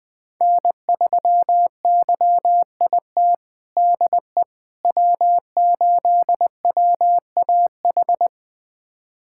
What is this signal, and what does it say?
Morse code, heard as a keyed tone, about 20 wpm, 700 Hz. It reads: N3YIT DE W8WAH
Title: USA Callsign DE Callsign - 5 Characters - 20wpm